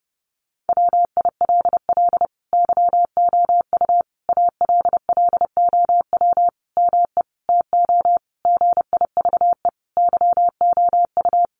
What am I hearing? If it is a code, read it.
Morse code, heard as a keyed tone, about 30 wpm, 700 Hz. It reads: WILL YOU ALLOW ME TO GIVE YOU
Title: Sentences from Top 1200 Words 30wpm